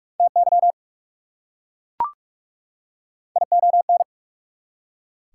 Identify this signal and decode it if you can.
Morse code, heard as a keyed tone, about 44 wpm, 690 Hz. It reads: TY ION
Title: Prefixes and Suffixes 45wpm